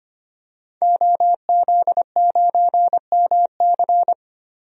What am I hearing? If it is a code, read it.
Morse code, heard as a keyed tone, about 25 wpm, 700 Hz. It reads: OZ9MC
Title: DXCC-100 Call Signs - 5 Characters - 25wpm